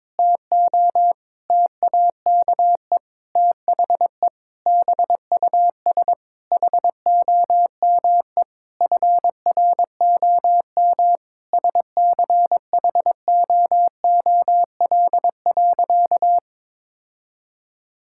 Morse code, heard as a keyed tone, about 22 wpm, 700 Hz. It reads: TO TAKE THE BUS HOME FROM SCHOOL.
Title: Sentences from Top 1000 Words - Review Part 6 - 22wpm